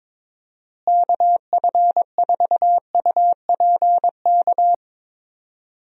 Morse code, heard as a keyed tone, about 22 wpm, 700 Hz. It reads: KF4UPK
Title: USA Call Signs - 6 Characters - Repeated 3x - 22wpm